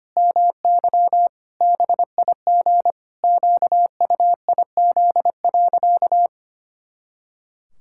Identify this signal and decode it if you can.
Morse code, heard as a keyed tone, about 25 wpm, 700 Hz. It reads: MY BIG QUIZ.
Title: Pangram Challenge v2 - 25wpm